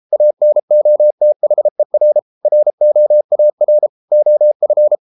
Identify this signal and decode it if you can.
Morse code, heard as a keyed tone, about 33 wpm, 600 Hz. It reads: ANOTHER ROAR OF